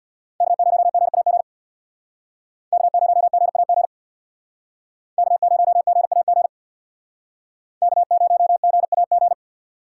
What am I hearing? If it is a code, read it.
Morse code, heard as a keyed tone, about 65 wpm, 700 Hz. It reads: K0GAG K0GAG K0GAG K0GAG